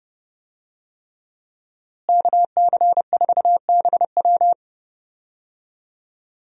Morse code, heard as a keyed tone, about 32 wpm, 700 Hz. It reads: KC4BW
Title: USA Call Signs - 5 Characters - Repeated 3x - 30wpm